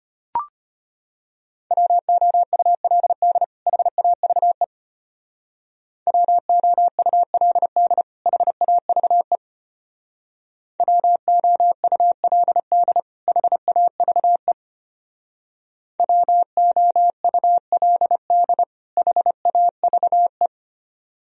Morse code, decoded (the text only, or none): WOULD HAVE WOULD HAVE WOULD HAVE WOULD HAVE